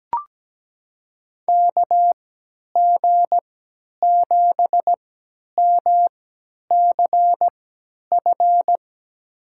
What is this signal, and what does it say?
Morse code, heard as a keyed tone, about 17 wpm, spaced as for 10 wpm, 700 Hz. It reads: KG7MCF
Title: USA Call Signs - 6 Characters - 3x Character Spacing - 17wpm